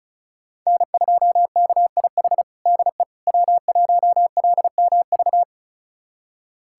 Morse code, heard as a keyed tone, about 36 wpm, 700 Hz. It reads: N2KIH DE W1LMV
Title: USA Callsign DE Callsign - 5 Characters - 35wpm